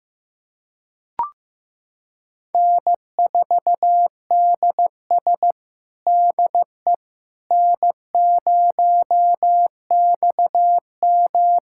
Morse code, heard as a keyed tone, about 15 wpm, 700 Hz. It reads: N4DS DE N0XM
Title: USA Callsign DE Callsign - 4 Characters - 15wpm